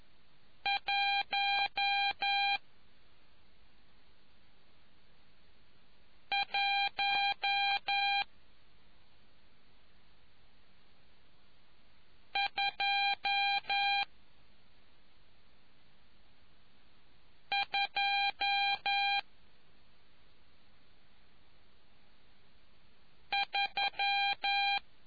Morse code, heard as a keyed tone, about 11 wpm, spaced as for 2 wpm, 800 Hz. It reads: 11223